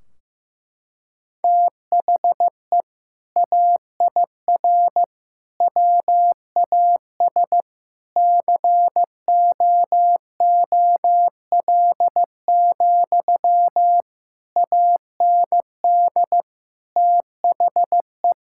Morse code, heard as a keyed tone, about 15 wpm, 700 Hz. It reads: THE AIR WAS COOL, AND THE
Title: Sentences from Top 1600 Words 15wpm